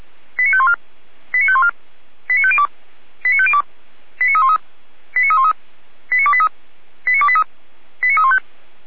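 Five-tone selective call (ZVEI), heard as quick runs of five tones, tones of about 70 ms. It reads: 89514 89514 89502 89502 89313 89313 89283 89283 89315
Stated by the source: ZVEI five-tone selective call 89340, received by SDR.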